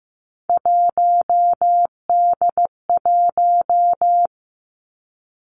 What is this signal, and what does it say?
Morse code, heard as a keyed tone, about 15 wpm, 700 Hz. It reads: 1D1
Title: Three Challenging Letter-Number Combinations 15wpm